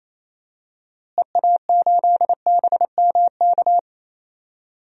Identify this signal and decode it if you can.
Morse code, heard as a keyed tone, about 27 wpm, 700 Hz. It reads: EA8BMK
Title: DXCC-100 Call Signs - 6 Characters - Repeated 2x - 28wpm